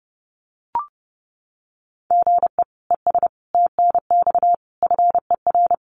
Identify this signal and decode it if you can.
Morse code, heard as a keyed tone, about 30 wpm, 700 Hz. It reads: GE ES TNX FER
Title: Common QSO Phrases Encore 30wpm